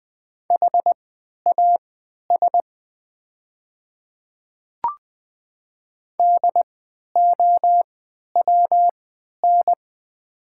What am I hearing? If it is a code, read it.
Morse code, heard as a keyed tone, about 20 wpm, spaced as for 11 wpm, 700 Hz. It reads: HAS DOWN
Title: 3x Character Spacing - Top 100 Words - 20wpm